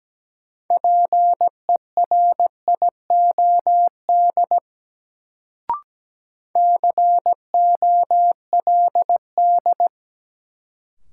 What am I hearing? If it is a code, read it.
Morse code, heard as a keyed tone, about 17 wpm, 700 Hz. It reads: PERIOD COLD
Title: Sets of 5 Words - New Word Drill - 17wpm